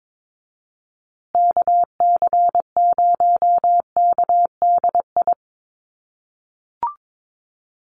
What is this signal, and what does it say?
Morse code, heard as a keyed tone, about 22 wpm, 700 Hz. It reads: KC0KDI
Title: USA Call Signs Repeated 3x 22wpm